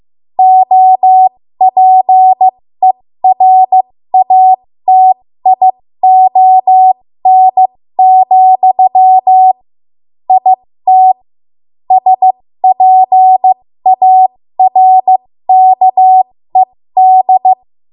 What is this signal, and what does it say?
Morse code, heard as a keyed tone, about 15 wpm, spaced as for 13 wpm, 750 Hz. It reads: OPERATION, IT SPARKED
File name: Morse / 13WPM